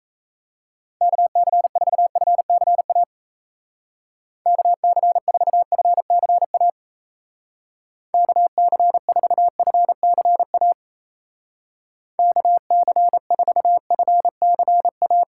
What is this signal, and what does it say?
Morse code, heard as a keyed tone, about 34 wpm, 700 Hz. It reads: KC4FCA KC4FCA KC4FCA KC4FCA
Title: Speed Racing - USA 2x3 Call Signs 28wpm